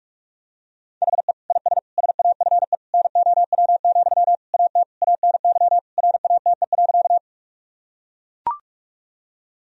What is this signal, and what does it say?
Morse code, heard as a keyed tone, about 41 wpm, 700 Hz. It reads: HE IS SAFE NOW, AT ANY RATE.
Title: Sentences from Top 1000 Words - Review Part 4 - 45wpm